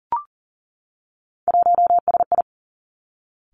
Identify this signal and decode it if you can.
Morse code, heard as a keyed tone, about 40 wpm, 700 Hz. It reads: 1SI
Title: Three Challenging Letter-Number Combinations 40wpm